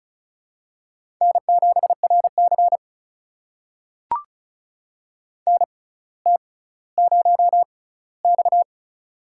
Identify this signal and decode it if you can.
Morse code, heard as a keyed tone, about 34 wpm, spaced as for 18 wpm, 700 Hz. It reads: N7RC NT0X
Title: USA Call Signs - 4 Characters - 6x Character Spacing - 35wpm